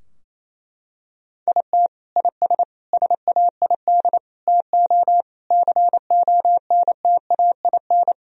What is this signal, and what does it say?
Morse code, heard as a keyed tone, about 28 wpm, 700 Hz. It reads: IT IS SAID TO CONTAIN
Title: Sentences from Top 1000 Words - Review Part 2 - 28wpm